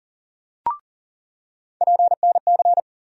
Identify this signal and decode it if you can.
Morse code, heard as a keyed tone, about 40 wpm, 700 Hz. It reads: PNC
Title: USA Call Sign Postfixes - 3 Letters 40wpm